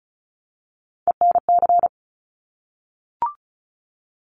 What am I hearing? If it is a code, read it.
Morse code, heard as a keyed tone, about 36 wpm, 700 Hz. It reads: ENC
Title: Top Three Letter Combinations in Words 35wpm